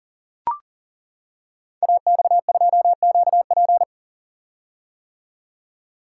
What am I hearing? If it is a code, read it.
Morse code, heard as a keyed tone, about 38 wpm, 700 Hz. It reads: AX2QP
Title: Code Groups - 5 Letter:Numbers - 40wpm